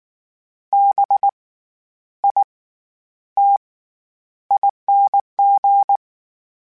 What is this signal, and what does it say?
Morse code, heard as a keyed tone, about 19 wpm, spaced as for 10 wpm, 800 Hz. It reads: BITING
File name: Special INGs / Special INGs-mp3s / SlowToFaster ing @18